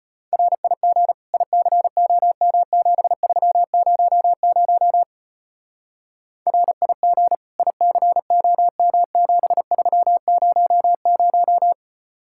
Morse code, heard as a keyed tone, about 32 wpm, 700 Hz. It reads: RIG ICOM7300 RIG ICOM7300